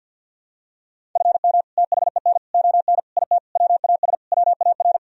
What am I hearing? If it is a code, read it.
Morse code, heard as a keyed tone, about 52 wpm, 700 Hz. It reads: FM THEN ON IT WAS WAR
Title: Sentences from Top 400 Words and Abbreviations 50wpm